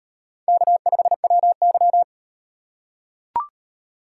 Morse code, heard as a keyed tone, about 38 wpm, 700 Hz. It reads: K5WY